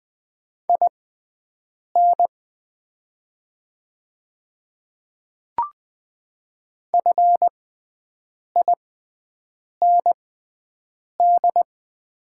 Morse code, heard as a keyed tone, about 20 wpm, spaced as for 7 wpm, 690 Hz. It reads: IN FIND
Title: 6x Character Spacing - Top 100 Words - 20wpm